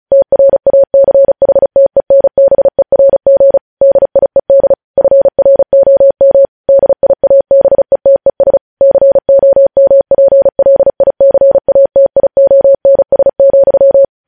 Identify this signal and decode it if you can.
Morse code, heard as a keyed tone, about 36 wpm, 550 Hz. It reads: TRACHTENBERG DIED FROM DIABETES COMPLICATIONS,
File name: podcast / News headlines-1x-35x35